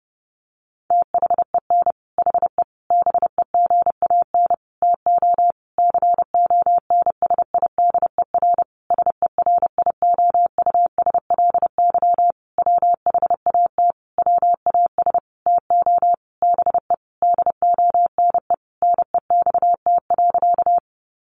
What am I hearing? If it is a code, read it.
Morse code, heard as a keyed tone, about 30 wpm, 700 Hz. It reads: THEN HE BEGAN TO CONSIDER SERIOUSLY WHAT WAS TO BE DONE NEXT.
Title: Sentences from Top 1200 Words 30wpm